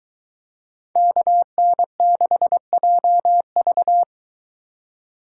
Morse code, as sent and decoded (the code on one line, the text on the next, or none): -.- -. -.... .--- ...-
KN6JV